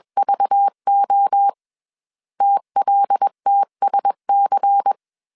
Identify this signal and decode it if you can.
Morse code, heard as a keyed tone, about 21 wpm, 800 Hz. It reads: VO TLTSC
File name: CW / DET / 1 6 1500C S C 20wpm 2116